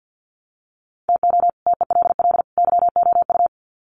Morse code, heard as a keyed tone, about 48 wpm, 700 Hz. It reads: TO TELL YOU
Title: Sets of 3 Words 50wpm